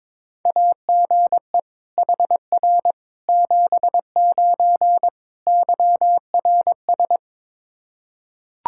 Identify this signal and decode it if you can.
Morse code, heard as a keyed tone, about 21 wpm, 700 Hz. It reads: AGE HR 79 YRS
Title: Practice QSO - Exchange 3 - More Info - 22wpm